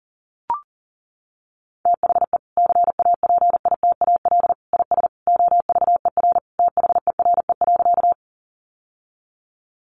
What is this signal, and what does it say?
Morse code, heard as a keyed tone, about 40 wpm, 700 Hz. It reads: THE CAPITAL IS OVER THERE.
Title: Sentences from Top 1000 Words 40wpm